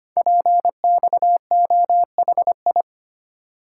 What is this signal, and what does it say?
Morse code, heard as a keyed tone, about 25 wpm, 700 Hz. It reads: PXOHI
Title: Code Groups - 5 Letters - 25wpm